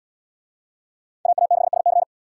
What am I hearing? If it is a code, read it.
Morse code, heard as a keyed tone, about 71 wpm, 700 Hz. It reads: AA6SC